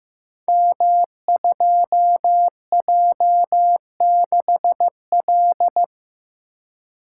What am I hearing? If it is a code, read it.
Morse code, heard as a keyed tone, about 15 wpm, 700 Hz. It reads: M2J6L
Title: Code Groups - 5 Letter:Numbers - 15wpm